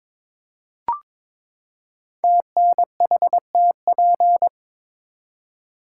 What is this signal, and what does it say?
Morse code, heard as a keyed tone, about 22 wpm, 700 Hz. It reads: TNHTP
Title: Code Groups - 5 Letters - 22wpm